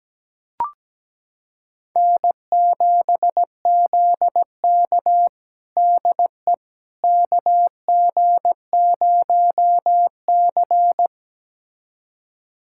Morse code, heard as a keyed tone, about 17 wpm, 700 Hz. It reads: N7ZK DE KG0C
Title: USA Callsign DE Callsign - 4 Characters - 17wpm